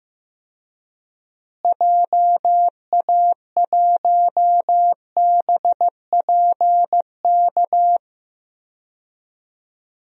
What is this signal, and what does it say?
Morse code, heard as a keyed tone, about 15 wpm, 700 Hz. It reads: JA1BPK
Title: DXCC-100 Call Signs - 6 Characters - Repeated 3x - 15wpm